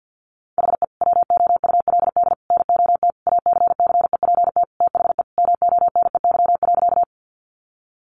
Morse code, heard as a keyed tone, about 50 wpm, 700 Hz. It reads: HE WOULD NOT ACCEPT THE MONEY.
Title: Sentences from Top 1300 Words 50wpm